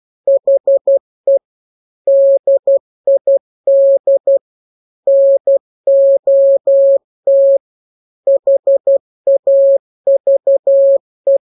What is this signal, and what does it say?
Morse code, heard as a keyed tone, about 12 wpm, 550 Hz. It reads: HE DID NOT HAVE
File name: jokes / joke 1 12 0000